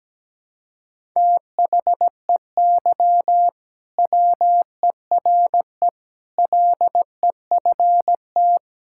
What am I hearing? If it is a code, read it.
Morse code, heard as a keyed tone, about 17 wpm, 700 Hz. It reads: THEY WERE LEFT
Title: Sentences from Top 1000 Words - Review Part 5 - 17wpm